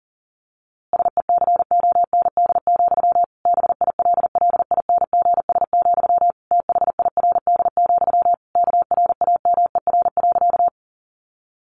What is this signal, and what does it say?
Morse code, heard as a keyed tone, about 40 wpm, 700 Hz. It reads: SECOND, BILLINGS, THIRD, KRAMER.
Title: A Sound Of Thunder 40wpm